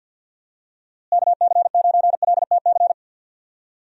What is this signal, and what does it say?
Morse code, heard as a keyed tone, about 50 wpm, 700 Hz. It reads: KC9LTC
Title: USA Call Signs - 6 Characters - Repeated 2x - 50wpm